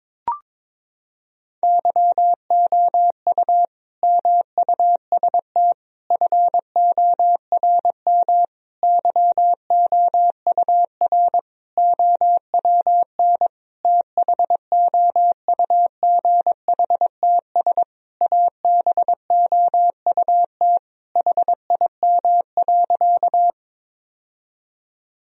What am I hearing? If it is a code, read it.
Morse code, heard as a keyed tone, about 22 wpm, 700 Hz. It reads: YOU MUST FORM YOUR OWN THOUGHTS ABOUT HIM.
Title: Sentences from Top 1000 Words - Review Part 1 - 22wpm